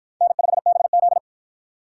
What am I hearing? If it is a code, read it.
Morse code, heard as a keyed tone, about 52 wpm, 700 Hz. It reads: N5BZ